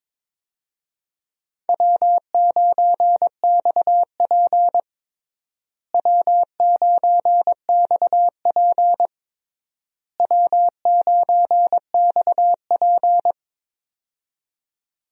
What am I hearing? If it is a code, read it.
Morse code, heard as a keyed tone, about 22 wpm, 700 Hz. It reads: W9XP W9XP W9XP
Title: USA Call Signs - 4 Characters - Repeated 3x - 22wpm